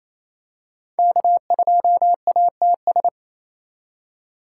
Morse code, heard as a keyed tone, about 28 wpm, 700 Hz. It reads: K2ATS
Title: USA Call Signs - 5 Characters - 2x Character Spacing - 28wpm